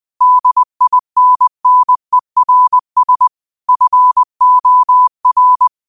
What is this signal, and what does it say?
Morse code, heard as a keyed tone, about 20 wpm, 1000 Hz. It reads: DINNERS FOR
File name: qotdfiles / 20wpm